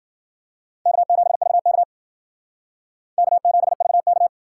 Morse code, heard as a keyed tone, about 51 wpm, 700 Hz. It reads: K7VK K7VK